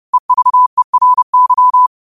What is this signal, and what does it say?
Morse code, heard as a keyed tone, about 31 wpm, 1000 Hz. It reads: EVERY
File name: qotdfiles / 30wpm